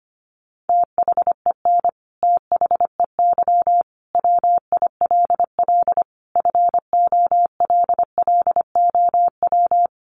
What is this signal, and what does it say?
Morse code, heard as a keyed tone, about 25 wpm, 700 Hz. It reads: THEN THEY WILL FOLLOW
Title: Sentences from Top 200 Words 25wpm